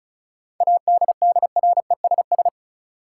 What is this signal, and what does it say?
Morse code, heard as a keyed tone, about 36 wpm, 700 Hz. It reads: ADDRESS
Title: Sets of 5 Words Encore - New Word Drill - 35wpm